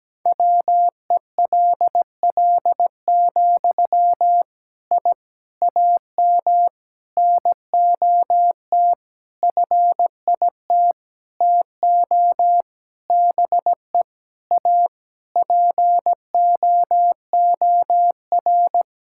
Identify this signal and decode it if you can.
Morse code, heard as a keyed tone, about 17 wpm, 700 Hz. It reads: WELL, I AM NOT FIT TO BE A POOR
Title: Sentences from Top 1000 Words - Review Part 5 - 17wpm